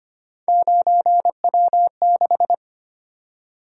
Morse code, heard as a keyed tone, about 25 wpm, 700 Hz. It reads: 9W6
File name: Three Letter-Number Combinations - 25wpm